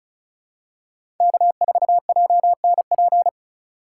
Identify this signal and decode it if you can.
Morse code, heard as a keyed tone, about 34 wpm, 700 Hz. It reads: K4JNP
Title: USA Call Signs - 3x Character Spacing - 35wpm